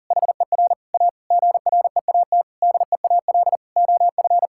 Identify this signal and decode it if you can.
Morse code, heard as a keyed tone, about 40 wpm, 700 Hz. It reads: HER A GREAT DEAL OF
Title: Sentences from Top 1000 Words - Review Part 6 - 40wpm